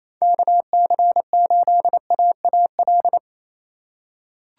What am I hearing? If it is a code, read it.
Morse code, heard as a keyed tone, about 28 wpm, 700 Hz. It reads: KC8AAL
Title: USA Call Signs - 6 Characters - 2x Character Spacing - 28wpm